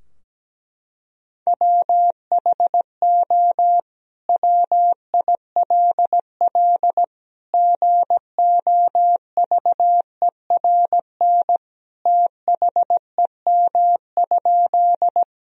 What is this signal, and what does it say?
Morse code, heard as a keyed tone, about 17 wpm, 700 Hz. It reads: WHO WILL GOVERN THEM?
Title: Sentences from Top 1000 Words - Review Part 2 - 17wpm